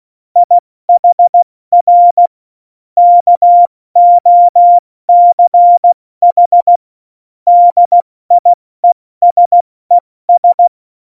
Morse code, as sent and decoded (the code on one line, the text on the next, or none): .. .... .-. / -.- --- -.-. .... / -.. .. ... ...
IHR KOCH DIESES